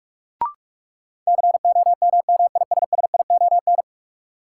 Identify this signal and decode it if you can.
Morse code, heard as a keyed tone, about 41 wpm, 700 Hz. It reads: COMMISSION
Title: Sentences from Top 1400 Words - New Word Drill - 45wpm